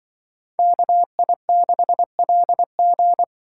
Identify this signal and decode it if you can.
Morse code, heard as a keyed tone, about 24 wpm, 700 Hz. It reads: KI6LG
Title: Speed Racing - USA 2x2 Call Signs 20wpm